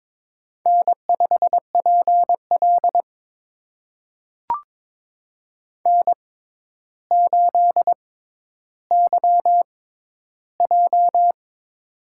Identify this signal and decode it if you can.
Morse code, heard as a keyed tone, about 22 wpm, spaced as for 11 wpm, 700 Hz. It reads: N5PL N8YJ
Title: USA Call Signs - 4 Characters - 6x Character Spacing - 22wpm